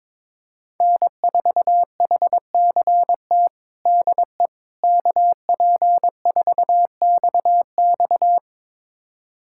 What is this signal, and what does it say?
Morse code, heard as a keyed tone, about 22 wpm, 700 Hz. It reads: N4HCT DE KP4XX
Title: USA Callsign DE Callsign - 5 Characters - 22wpm